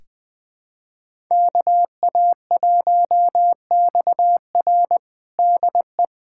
Morse code, heard as a keyed tone, about 21 wpm, 700 Hz. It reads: KA1XR DE
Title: USA Callsign DE Callsign - 5 Characters - 20wpm